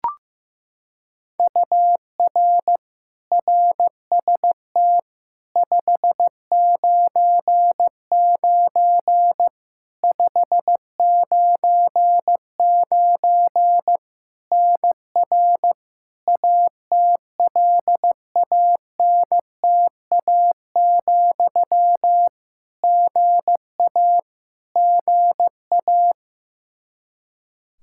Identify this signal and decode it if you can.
Morse code, heard as a keyed tone, about 15 wpm, 700 Hz. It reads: UR RST 599 599 NR ATLANTA, GA GA
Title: Practice QSO - Exchange 1 - Beginning - 15wpm